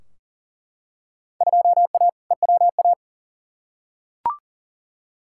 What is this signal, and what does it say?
Morse code, heard as a keyed tone, about 40 wpm, 700 Hz. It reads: 2A EWA
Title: Field Day Exchanges - 40wpm